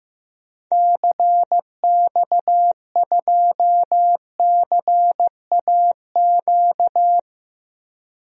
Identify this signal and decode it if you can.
Morse code, heard as a keyed tone, about 15 wpm, 700 Hz. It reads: CX2CAQ
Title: DXCC-100 Call Signs - 6 Characters - 15wpm